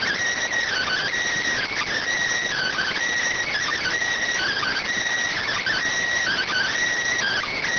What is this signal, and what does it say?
Emergency Alert System (SAME) tones (header bursts), heard as harsh buzzing bursts. It is heard from 0.7 to 6.7 s.